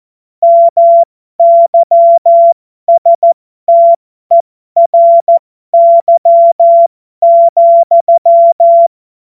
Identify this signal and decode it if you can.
Morse code, heard as a keyed tone, about 14 wpm, spaced as for 12 wpm, 680 Hz. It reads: MYSTERY,
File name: Morsekurs / Witze